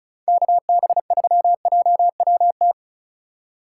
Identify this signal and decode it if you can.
Morse code, heard as a keyed tone, about 36 wpm, 700 Hz. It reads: KB3JWT
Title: USA Call Signs - 3x Character Spacing - 35wpm